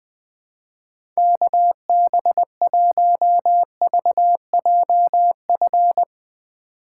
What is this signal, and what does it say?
Morse code, heard as a keyed tone, about 20 wpm, 700 Hz. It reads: KB1VJF
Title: USA Call Signs - 6 Characters - 3x Character Spacing - 20wpm